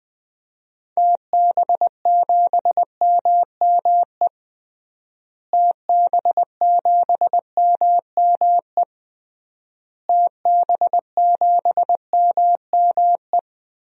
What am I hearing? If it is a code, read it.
Morse code, heard as a keyed tone, about 21 wpm, 700 Hz. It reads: TB7MME TB7MME TB7MME
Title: DXCC-100 Call Signs - 6 Characters - Repeated 3x - 20wpm